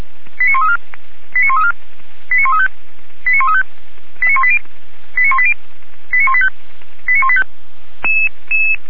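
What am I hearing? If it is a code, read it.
Five-tone selective call (ZVEI), heard as quick runs of five tones, tones of about 80 ms. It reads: 89134 89134 89135 89135 89180 89180 89185 89185
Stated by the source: ZVEI five-tone fire brigade call 89132 received on an SDR, logged by Feuerwehr Orlamünde.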